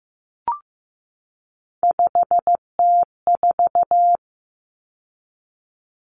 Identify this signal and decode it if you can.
Morse code, heard as a keyed tone, about 15 wpm, 700 Hz. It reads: 5T4
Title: Common Cut Numbers - 3-Digit 15wpm